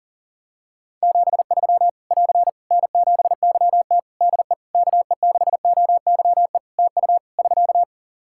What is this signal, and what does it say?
Morse code, heard as a keyed tone, about 37 wpm, 700 Hz. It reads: 73 <AR> N7YT DE KE6OYE TU <SK>
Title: Practice QSO - Exchange 4 - Ending - 40wpm